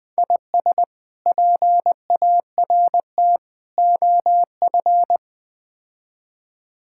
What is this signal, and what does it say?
Morse code, heard as a keyed tone, about 20 wpm, 700 Hz. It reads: IS PART OF
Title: Build A Phrase - 4-Word Phrases - 20wpm